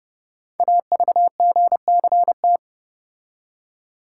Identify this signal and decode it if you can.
Morse code, heard as a keyed tone, about 30 wpm, 700 Hz. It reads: AVGCT
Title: Code Groups - 5 Letter:Numbers - 30wpm